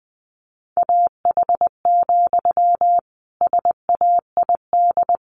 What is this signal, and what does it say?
Morse code, heard as a keyed tone, about 20 wpm, 700 Hz. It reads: AH, SAID
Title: A Sound Of Thunder 20wpm